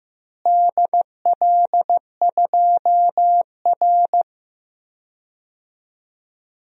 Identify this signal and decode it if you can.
Morse code, heard as a keyed tone, about 15 wpm, 700 Hz. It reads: DL2R
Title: DXCC-100 Call Signs - 4 Characters - Repeated 3x - 15wpm